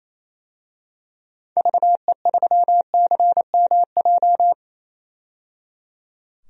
Morse code, heard as a keyed tone, about 28 wpm, 700 Hz. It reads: VE3CMJ
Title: DXCC-100 Call Signs - 6 Characters - 28wpm